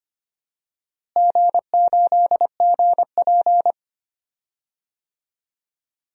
Morse code, heard as a keyed tone, about 25 wpm, 700 Hz. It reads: G8GP
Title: DXCC-100 Call Signs - 4 Characters - Repeated 2x - 25wpm